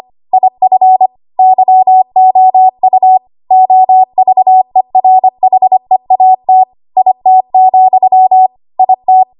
Morse code, heard as a keyed tone, about 25 wpm, 750 Hz. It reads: IF YOU OVERHEAT IT, IT
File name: Archive / 25 WPM / 25WPM